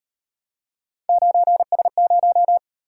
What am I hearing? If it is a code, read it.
Morse code, heard as a keyed tone, about 38 wpm, 700 Hz. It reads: 9S0